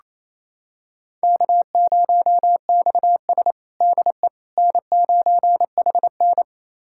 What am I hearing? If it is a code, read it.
Morse code, heard as a keyed tone, about 27 wpm, 700 Hz. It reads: K0XS DE N9HN
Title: USA Callsign DE Callsign - 4 Characters - 28wpm